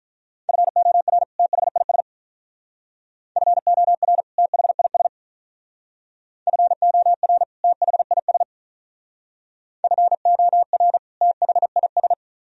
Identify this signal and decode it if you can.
Morse code, heard as a keyed tone, about 42 wpm, 700 Hz. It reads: FOR THIS FOR THIS FOR THIS FOR THIS
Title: Speed-Racing - Sets of 2 Words 35wpm